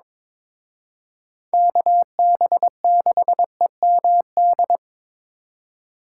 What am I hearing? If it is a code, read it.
Morse code, heard as a keyed tone, about 26 wpm, 700 Hz. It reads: KB6EMD
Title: Speed Racing - USA 2x3 Call Signs 22wpm